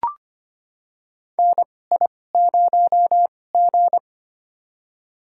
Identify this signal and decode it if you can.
Morse code, heard as a keyed tone, about 25 wpm, spaced as for 18 wpm, 700 Hz. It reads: NI0G
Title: USA Call Signs - 2x Character Spacing - 25wpm